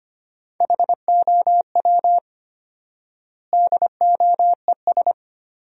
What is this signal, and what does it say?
Morse code, heard as a keyed tone, about 25 wpm, 700 Hz. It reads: HOW DOES
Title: 4x Word-Spacing - Sentences from Top 200 Words - 25wpm